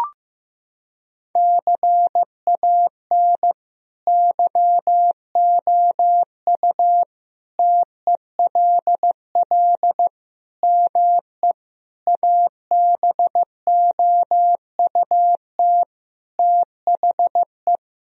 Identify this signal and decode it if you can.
Morse code, heard as a keyed tone, about 16 wpm, 700 Hz. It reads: CAN YOU TELL ME ABOUT THE
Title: Sentences from Top 1000 Words - Review Part 3 - 15wpm